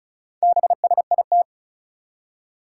Morse code, heard as a keyed tone, about 36 wpm, 700 Hz. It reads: BSIT
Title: Four Challenging Letter-Number Combinations 35wpm